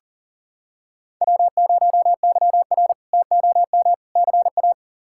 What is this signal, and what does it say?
Morse code, heard as a keyed tone, about 40 wpm, 700 Hz. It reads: W0YR TOM CA
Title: K1USN SST QSO Practice - 40wpm